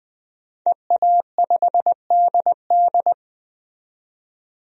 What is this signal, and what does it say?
Morse code, heard as a keyed tone, about 20 wpm, 700 Hz. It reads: EA5DD
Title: DXCC-100 Call Signs - 5 Characters - 20wpm